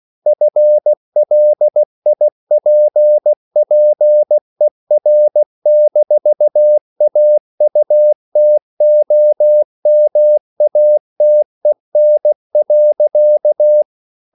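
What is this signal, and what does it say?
Morse code, heard as a keyed tone, about 16 wpm, 600 Hz. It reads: FLIPPER-AUTOMATEN.
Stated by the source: Slow (QRS) Morse code at 16 wpm, from CW med Gnister.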